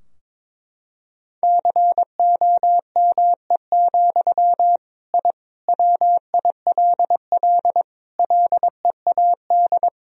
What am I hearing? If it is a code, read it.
Morse code, heard as a keyed tone, about 21 wpm, 700 Hz. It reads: COME, I WILL LEAD
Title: Sentences from Top 1000 Words - Review Part 2 - 22wpm